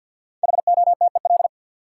Morse code, heard as a keyed tone, about 46 wpm, 700 Hz. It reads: HOTEL
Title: Sentences from Top 1600 Words - New Word Drill Alt - 50wpm